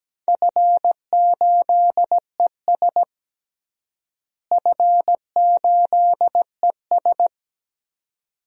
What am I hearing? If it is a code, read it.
Morse code, heard as a keyed tone, about 17 wpm, 700 Hz. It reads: F8ES F8ES
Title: DXCC-100 Call Signs - 4 Characters - Repeated 2x - 17wpm